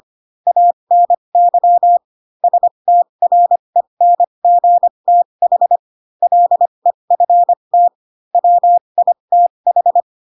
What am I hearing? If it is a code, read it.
Morse code, heard as a keyed tone, about 25 wpm, spaced as for 22 wpm, 700 Hz. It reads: ANY STRENGTH LEFT WITH
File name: Pinocchio / Pinocchio ch6 18 25 ik1hge